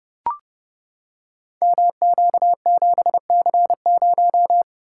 Morse code, heard as a keyed tone, about 30 wpm, 700 Hz. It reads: MQ7C0
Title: Code Groups - 5 Letter:Numbers - 30wpm